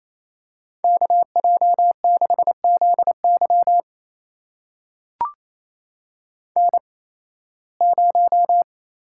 Morse code, heard as a keyed tone, about 28 wpm, spaced as for 18 wpm, 700 Hz. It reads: KJ6ZY N0
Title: USA Call Signs - 8x Character Spacing - 28wpm